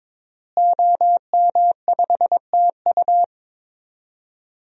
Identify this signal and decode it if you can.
Morse code, heard as a keyed tone, about 22 wpm, 700 Hz. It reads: OM5TU
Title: DXCC-100 Call Signs - 5 Characters - Repeated 3x - 22wpm